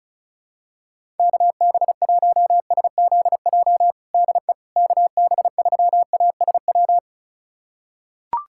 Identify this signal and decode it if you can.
Morse code, heard as a keyed tone, about 36 wpm, 700 Hz. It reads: KB1SZJ DE KB3ASW
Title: USA Callsign DE Callsign - 6 Characters - 35wpm